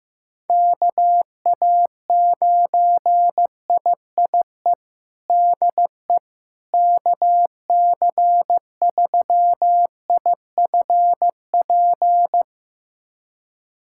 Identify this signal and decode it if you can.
Morse code, heard as a keyed tone, about 15 wpm, 700 Hz. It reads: KA9IIE DE KC3IFP
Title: USA Callsign DE Callsign 15wpm